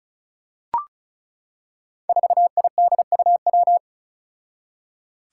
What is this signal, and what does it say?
Morse code, heard as a keyed tone, about 34 wpm, 700 Hz. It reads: 4IDUW
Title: Five Challenging Letter-Number Combinations 35wpm